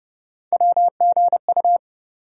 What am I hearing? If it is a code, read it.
Morse code, heard as a keyed tone, about 30 wpm, 700 Hz. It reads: WGU
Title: USA Call Sign Postfixes - 3 Letters 30wpm